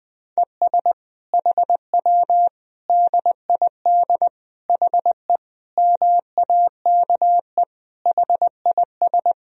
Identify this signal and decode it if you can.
Morse code, heard as a keyed tone, about 20 wpm, 700 Hz. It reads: ES HW DID HE MAKE HIS
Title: Sentences from Top 400 Words and Abbreviations 20wpm